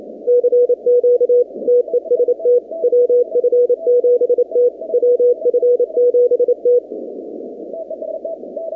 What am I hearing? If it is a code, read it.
Morse code, heard as a keyed tone, about 28 wpm, 490 Hz. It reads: CQ TEST WF7T WF7T
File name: 2019-cw-wpx-kt5j-k5pi